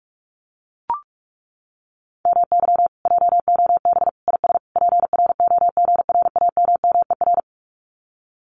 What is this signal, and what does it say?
Morse code, heard as a keyed tone, about 44 wpm, 700 Hz. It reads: MY JOB IS PROGRAMMER
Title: Common QSO Phrases Encore 45wpm